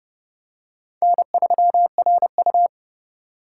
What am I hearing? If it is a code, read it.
Morse code, heard as a keyed tone, about 29 wpm, 700 Hz. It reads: N3RU
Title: USA Call Signs - 4 Characters - 2x Character Spacing - 30wpm